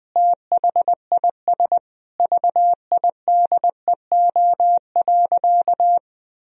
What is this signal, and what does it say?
Morse code, heard as a keyed tone, about 21 wpm, 700 Hz. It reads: THIS VIDEO.
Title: Sentences from Top 1000 Words - Review Part 4 - 20wpm